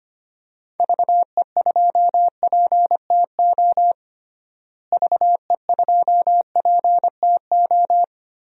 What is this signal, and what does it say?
Morse code, heard as a keyed tone, about 25 wpm, 700 Hz. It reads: VE2PTO VE2PTO
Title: DXCC-100 Call Signs - 6 Characters - Repeated 3x - 25wpm